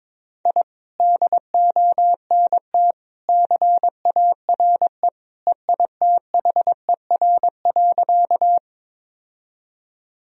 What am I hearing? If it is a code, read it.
Morse code, heard as a keyed tone, about 22 wpm, 700 Hz. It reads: I DONT CARE EITHER.
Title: Sentences from Top 1000 Words - Review Part 4 - 22wpm